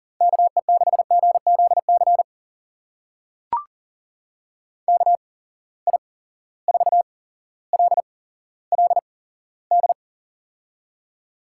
Morse code, heard as a keyed tone, about 40 wpm, spaced as for 17 wpm, 700 Hz. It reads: KE6GZC KI4LLD
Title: USA Call Signs - 6 Characters - 8x Character Spacing - 40wpm